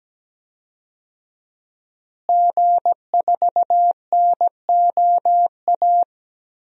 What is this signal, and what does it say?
Morse code, heard as a keyed tone, about 17 wpm, 700 Hz. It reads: G4NOA
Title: DXCC-100 Call Signs - 5 Characters - Repeated 3x - 17wpm